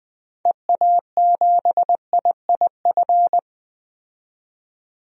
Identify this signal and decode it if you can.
Morse code, heard as a keyed tone, about 20 wpm, 700 Hz. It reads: EA7IIF
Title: DXCC-100 Call Signs - 6 Characters - 20wpm